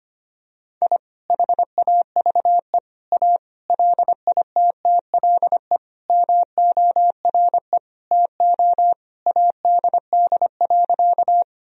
Morse code, heard as a keyed tone, about 25 wpm, 700 Hz. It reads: I HAVE A LITTLE MORE TO ADD.
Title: Sentences from Top 1000 Words - Review Part 1 - 25wpm